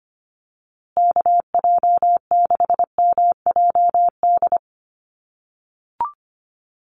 Morse code, heard as a keyed tone, about 25 wpm, 700 Hz. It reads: KJ6MJD
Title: USA Call Signs Repeated 3x 25wpm